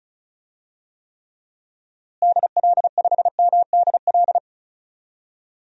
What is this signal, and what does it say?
Morse code, heard as a keyed tone, about 36 wpm, 700 Hz. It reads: DL5MDL
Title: DXCC-100 Call Signs - 6 Characters - Repeated 2x - 35wpm